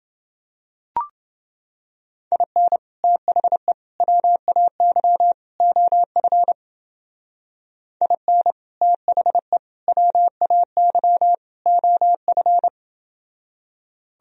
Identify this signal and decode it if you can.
Morse code, heard as a keyed tone, about 28 wpm, 700 Hz. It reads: IN THE WAY OF IN THE WAY OF